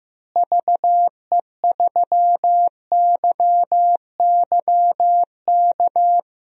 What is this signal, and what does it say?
Morse code, heard as a keyed tone, about 15 wpm, 700 Hz. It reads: VE3YYK
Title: DXCC-100 Call Signs - 6 Characters - Repeated 3x - 15wpm